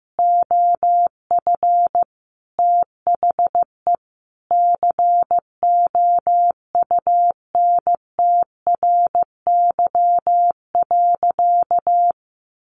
Morse code, heard as a keyed tone, about 15 wpm, 700 Hz. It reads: OF THE COUNTRY.
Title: Sentences from Top 300 Words 15wpm